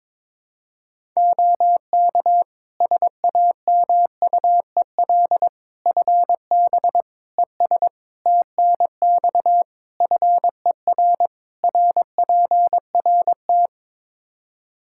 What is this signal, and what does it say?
Morse code, heard as a keyed tone, about 22 wpm, 700 Hz. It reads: OK SAMUEL FB ES TNX FER RPRT
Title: Practice QSO - Exchange 2 - Info - 22wpm